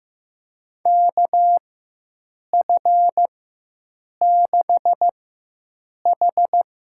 Morse code, heard as a keyed tone, about 15 wpm, spaced as for 7 wpm, 700 Hz. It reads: KF6H
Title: USA Call Signs - 4x Character Spacing - 15wpm